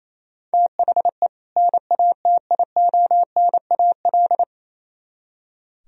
Morse code, heard as a keyed tone, about 36 wpm, 700 Hz. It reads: THE NATIONAL